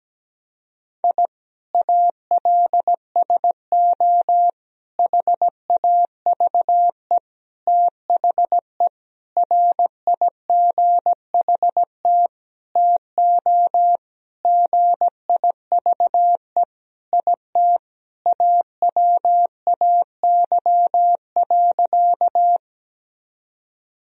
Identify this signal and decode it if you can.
Morse code, heard as a keyed tone, about 17 wpm, 700 Hz. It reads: I ALSO HAVE THE RIGHT TO GIVE IT AWAY.
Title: Sentences from Top 1000 Words - Review Part 1 - 17wpm